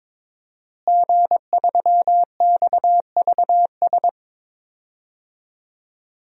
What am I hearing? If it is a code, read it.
Morse code, heard as a keyed tone, about 22 wpm, 700 Hz. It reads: G3XVS
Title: DXCC-100 Call Signs - 5 Characters - Repeated 2x - 22wpm